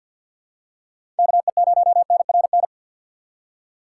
Morse code, heard as a keyed tone, about 53 wpm, 700 Hz. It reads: KE0NRN
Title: USA Call Signs - 2x Character Spacing - 50wpm